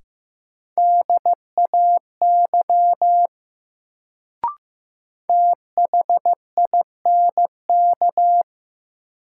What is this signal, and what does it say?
Morse code, heard as a keyed tone, about 15 wpm, 700 Hz. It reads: DAY THINK
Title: Sets of 2 Words - New Word Drill - 15wpm